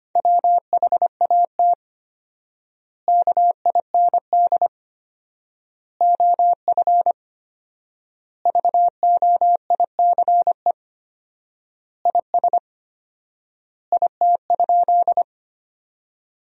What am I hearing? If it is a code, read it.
Morse code, heard as a keyed tone, about 25 wpm, 700 Hz. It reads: WHAT KIND OF VOICE IS IT?
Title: 4x Word-Spacing - Sentences from Top 500 Words - 25wpm